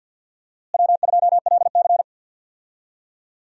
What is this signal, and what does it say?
Morse code, heard as a keyed tone, about 52 wpm, 700 Hz. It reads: W2LC
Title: USA Call Signs - 4 Characters - Repeated 2x - 50wpm